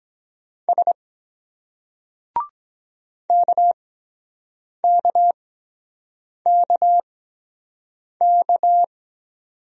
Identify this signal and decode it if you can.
Morse code, heard as a keyed tone, about 23 wpm, spaced as for 7 wpm, 700 Hz. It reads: S KKKK